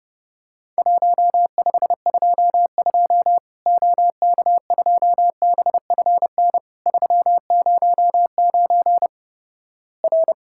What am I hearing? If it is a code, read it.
Morse code, heard as a keyed tone, about 32 wpm, 700 Hz. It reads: 1522 OK2BFN 309 R
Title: Worked All Europe DX Contest - QTC Exchange - 30wpm